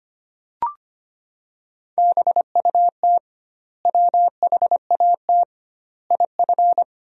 Morse code, heard as a keyed tone, about 25 wpm, 700 Hz. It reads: BUT WHAT IF
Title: 2x Word-Spacing - Sentences from Top 100 Words - 25wpm